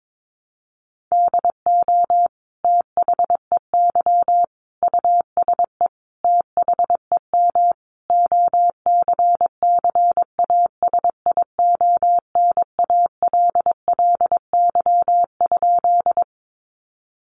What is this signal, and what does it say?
Morse code, heard as a keyed tone, about 22 wpm, 700 Hz. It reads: DO THEY USE THEM OCCASIONALLY?
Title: Sentences from Top 1300 Words 22wpm